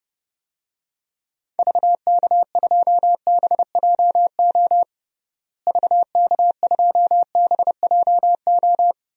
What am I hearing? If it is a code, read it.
Morse code, heard as a keyed tone, about 32 wpm, 700 Hz. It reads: VK2BJO VK2BJO
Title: DXCC-100 Call Signs - 6 Characters - Repeated 2x - 30wpm